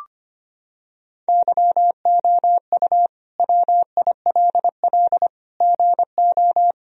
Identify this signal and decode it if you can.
Morse code, heard as a keyed tone, about 25 wpm, 700 Hz. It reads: YOU WILL GO
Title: Sentences from Top 1000 Words - Review Part 2 - 25wpm